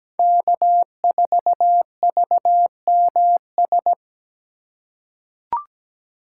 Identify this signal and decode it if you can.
Morse code, heard as a keyed tone, about 17 wpm, 700 Hz. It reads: K4VMS
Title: USA Call Signs - 5 Characters - 3x Character Spacing - 17wpm